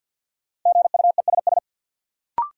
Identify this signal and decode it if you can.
Morse code, heard as a keyed tone, about 53 wpm, 700 Hz. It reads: GUESS
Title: Sentences from Top 1500 Words - New Word Drill - 50wpm